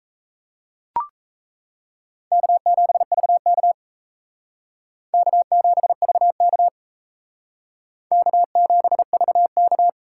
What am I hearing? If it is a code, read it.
Morse code, heard as a keyed tone, about 38 wpm, 700 Hz. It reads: K7VK K7VK K7VK